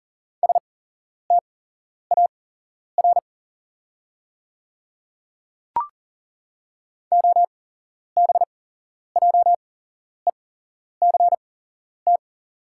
Morse code, heard as a keyed tone, about 40 wpm, spaced as for 11 wpm, 690 Hz. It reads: STAR OBJECT
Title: 8x Character Spacing - Top 500 Words - 40wpm